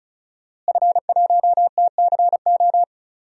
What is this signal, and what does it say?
Morse code, heard as a keyed tone, about 34 wpm, 700 Hz. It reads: F1TCO
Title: DXCC-100 Call Signs - 5 Characters - Repeated 3x - 35wpm